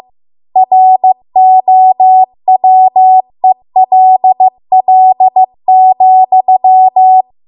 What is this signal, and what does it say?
Morse code, heard as a keyed tone, about 15 wpm, 750 Hz. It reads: ROWELL,